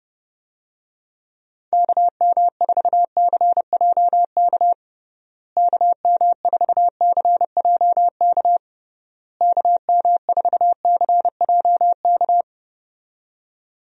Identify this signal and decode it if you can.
Morse code, heard as a keyed tone, about 30 wpm, 700 Hz. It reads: KM4CJK KM4CJK KM4CJK
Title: USA Call Signs - 6 Characters - Repeated 3x - 30wpm